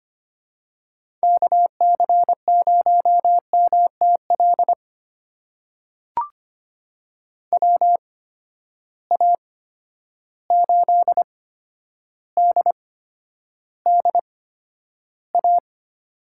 Morse code, heard as a keyed tone, about 25 wpm, spaced as for 11 wpm, 700 Hz. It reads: KC0MTL WA8DDA
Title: USA Call Signs - 6 Characters - 8x Character Spacing - 25wpm